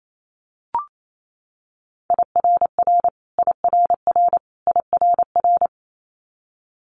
Morse code, heard as a keyed tone, about 28 wpm, 700 Hz. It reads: IRR IRR IRR
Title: Multi-letter Phonemes Repeated 3x 28wpm